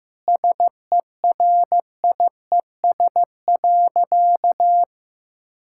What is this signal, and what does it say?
Morse code, heard as a keyed tone, about 15 wpm, 700 Hz. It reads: SERIES.
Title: Sentences from Top 1000 Words - Review Part 5 - 15wpm